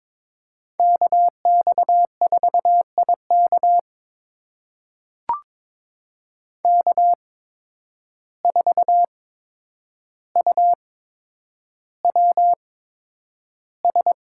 Morse code, heard as a keyed tone, about 22 wpm, spaced as for 9 wpm, 700 Hz. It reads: KX4IK K4UWS
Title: USA Call Signs - 5 Characters - 8x Character Spacing - 22wpm